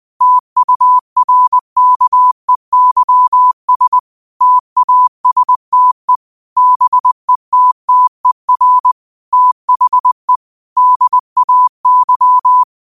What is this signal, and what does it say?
Morse code, heard as a keyed tone, about 20 wpm, 1000 Hz. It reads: TURKEYS TASTE BETTER THE DAY
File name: qotdfiles / 20wpm